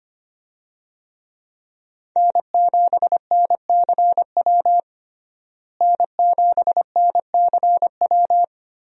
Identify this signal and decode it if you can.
Morse code, heard as a keyed tone, about 25 wpm, 700 Hz. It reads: N7NCW N7NCW
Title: USA Call Signs - 5 Characters - Repeated 2x - 25wpm